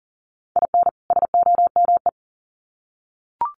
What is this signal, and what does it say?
Morse code, heard as a keyed tone, about 40 wpm, 700 Hz. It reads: IN SOME
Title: Sets of 2 Words 40wpm